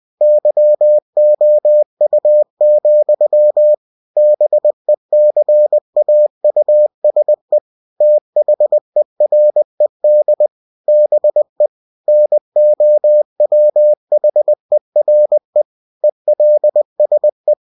Morse code, heard as a keyed tone, about 20 wpm, 600 Hz. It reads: YOU, BECAUSE THERED BE NOWHERE ELSE
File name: cw-books / winnie-the-pooh / winnie-20wpm-page-0085